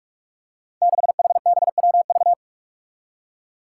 Morse code, heard as a keyed tone, about 44 wpm, 700 Hz. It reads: 6HBWV
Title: Five Challenging Letter-Number Combinations 45wpm